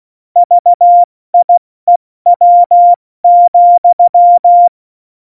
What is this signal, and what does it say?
Morse code, heard as a keyed tone, about 16 wpm, 700 Hz. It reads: VIEW,
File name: Morsekurs / Witze / Witz 25 Juli 17, Tempo 70 F80